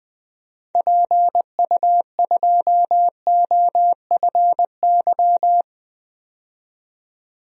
Morse code, heard as a keyed tone, about 20 wpm, 700 Hz. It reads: PU2OFY
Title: DXCC-100 Call Signs - 6 Characters - 20wpm